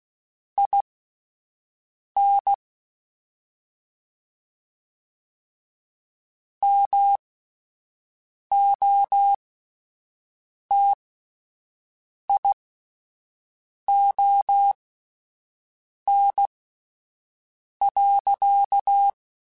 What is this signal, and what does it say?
Morse code, heard as a keyed tone, about 16 wpm, spaced as for 5 wpm, 800 Hz. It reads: IN MOTION.